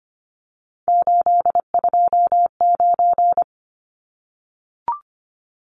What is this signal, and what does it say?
Morse code, heard as a keyed tone, about 25 wpm, 700 Hz. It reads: 829
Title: Numbers - 3-Digit 25wpm